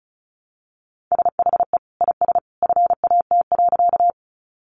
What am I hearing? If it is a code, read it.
Morse code, heard as a keyed tone, about 34 wpm, 700 Hz. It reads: SHE IS FAT.
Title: Sentences from Top 600 Words 35wpm